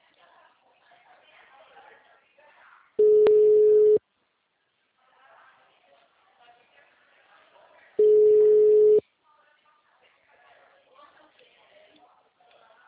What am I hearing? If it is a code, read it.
Ringback tone (Europe, 425 Hz), with speech.
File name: out- -2006